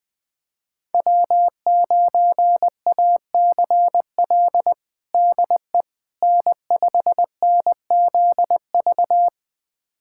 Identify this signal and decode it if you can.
Morse code, heard as a keyed tone, about 21 wpm, 700 Hz. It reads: W9ACL DE N5NZV
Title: USA Callsign DE Callsign - 5 Characters - 20wpm